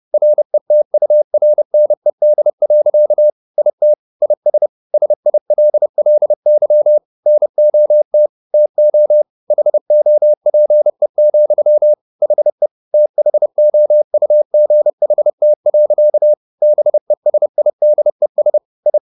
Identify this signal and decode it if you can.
Morse code, heard as a keyed tone, about 30 wpm, 600 Hz. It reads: RETURNED. IT IS SILLY NOT TO HOPE, HE THOUGHT. BESIDES I